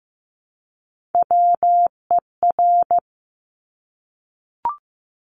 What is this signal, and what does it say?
Morse code, heard as a keyed tone, about 15 wpm, 700 Hz. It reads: WER
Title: Top Three Letter Combinations in Words 15wpm